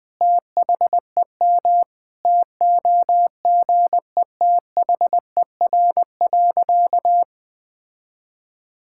Morse code, heard as a keyed tone, about 20 wpm, 700 Hz. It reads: THEM TOGETHER.
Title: Sentences from Top 1600 Words 20wpm